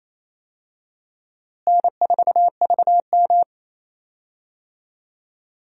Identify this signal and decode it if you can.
Morse code, heard as a keyed tone, about 28 wpm, 700 Hz. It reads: N4VM
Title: USA Call Signs - 4 Characters - Repeated 3x - 28wpm